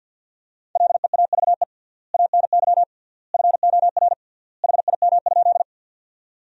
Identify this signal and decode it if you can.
Morse code, heard as a keyed tone, about 48 wpm, 700 Hz. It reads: LEAVE ANY FOR HIM?
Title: 3x Word-Spacing - Sentences from Top 400 Words - 50wpm